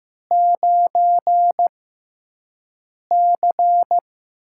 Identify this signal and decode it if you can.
Morse code, heard as a keyed tone, about 15 wpm, spaced as for 5 wpm, 700 Hz. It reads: 9C
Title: USA Call Signs - 4 Characters - 6x Character Spacing - 15wpm